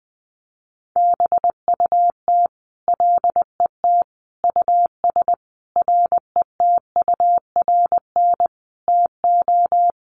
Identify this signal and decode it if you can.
Morse code, heard as a keyed tone, about 20 wpm, 700 Hz. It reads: BUT LET US RETURN TO
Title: Sentences from Top 800 Words 20wpm